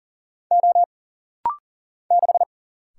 Morse code, heard as a keyed tone, about 39 wpm, 700 Hz. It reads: O 6
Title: Single Letter-Number - Rapid-Fire 40wpm